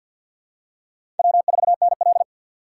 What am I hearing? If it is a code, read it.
Morse code, heard as a keyed tone, about 53 wpm, 700 Hz. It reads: W4NL
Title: USA Call Signs - 4x Character Spacing - 50wpm